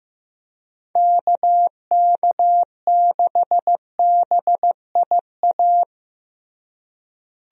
Morse code, heard as a keyed tone, about 16 wpm, 700 Hz. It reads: KK6BIA
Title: USA Call Signs - 6 Characters - Repeated 3x - 15wpm